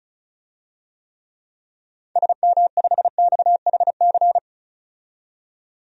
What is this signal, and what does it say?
Morse code, heard as a keyed tone, about 36 wpm, 700 Hz. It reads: SM5XHC
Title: DXCC-100 Call Signs - 6 Characters - 35wpm